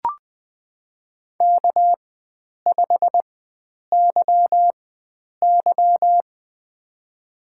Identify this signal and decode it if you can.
Morse code, heard as a keyed tone, about 20 wpm, spaced as for 9 wpm, 700 Hz. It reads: K5YY
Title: USA Call Signs - 4 Characters - 4x Character Spacing - 20wpm